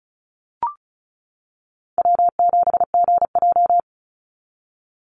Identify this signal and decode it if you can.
Morse code, heard as a keyed tone, about 34 wpm, 700 Hz. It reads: W7GJ
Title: NA SOTA Chaser Callsigns 35wpm